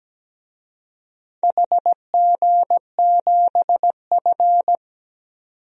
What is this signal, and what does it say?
Morse code, heard as a keyed tone, about 17 wpm, 700 Hz. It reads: HG7F
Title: DXCC-100 Call Signs - 4 Characters - 17wpm